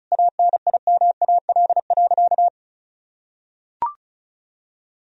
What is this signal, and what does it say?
Morse code, heard as a keyed tone, about 34 wpm, 700 Hz. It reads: ANIMAL.
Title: Sentences from Top 1000 Words - Review Part 1 - 35wpm